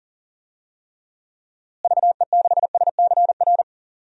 Morse code, heard as a keyed tone, about 40 wpm, 700 Hz. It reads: VE6SCR
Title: DXCC-100 Call Signs - 6 Characters - Repeated 3x - 40wpm